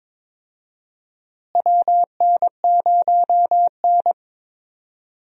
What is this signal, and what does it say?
Morse code, heard as a keyed tone, about 22 wpm, 700 Hz. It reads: WN0N
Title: USA Call Signs - 4 Characters - Repeated 2x - 22wpm